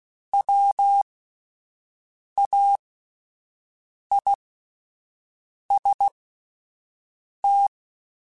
Morse code, heard as a keyed tone, about 16 wpm, spaced as for 5 wpm, 800 Hz. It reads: WAIST